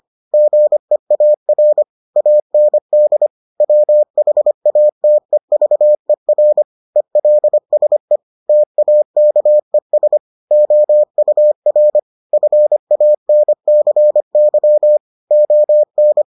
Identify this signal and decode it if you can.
Morse code, heard as a keyed tone, about 25 wpm, 600 Hz. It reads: GEAR AND WHATEVER ELSE TAKES OUR FANCY ON